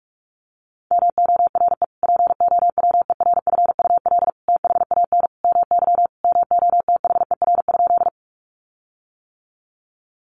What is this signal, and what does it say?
Morse code, heard as a keyed tone, about 46 wpm, 700 Hz. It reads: MORE POWERFUL THAN MY MOTHER?
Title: Sentences from Top 1300 Words 45wpm